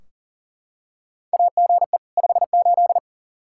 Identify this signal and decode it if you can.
Morse code, heard as a keyed tone, about 40 wpm, 700 Hz. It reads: AGE 58
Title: Practice QSO - Exchange 3 - More Info - 40wpm